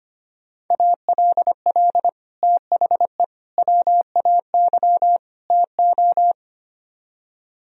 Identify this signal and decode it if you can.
Morse code, heard as a keyed tone, about 32 wpm, 700 Hz. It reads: ALL THE WAY TO